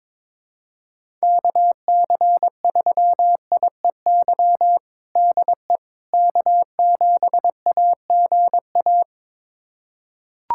Morse code, heard as a keyed tone, about 22 wpm, 700 Hz. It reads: KC3IEY DE K7AGA
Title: USA Callsign DE Callsign 22wpm